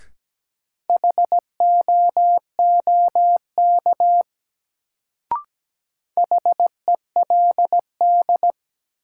Morse code, heard as a keyed tone, about 17 wpm, 700 Hz. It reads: HOOK HELD
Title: Sentences from Top 1600 Words - New Word Drill - 17wpm